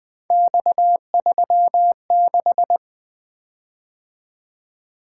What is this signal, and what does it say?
Morse code, heard as a keyed tone, about 20 wpm, 700 Hz. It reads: X36
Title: Three Letter-Number Combinations - 20wpm